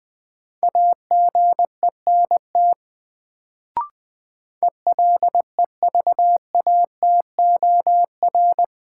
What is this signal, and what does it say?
Morse code, heard as a keyed tone, about 20 wpm, 700 Hz. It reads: AGENT ELEVATOR
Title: Sentences from Top 1600 Words - New Word Drill Alt - 20wpm